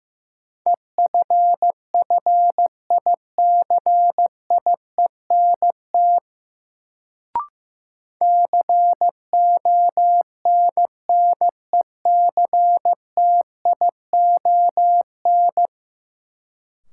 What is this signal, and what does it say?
Morse code, heard as a keyed tone, about 15 wpm, 700 Hz. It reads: EFFICIENT CONNECTION
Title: Sentences from Top 1300 Words - New Word Drill Alt - 15wpm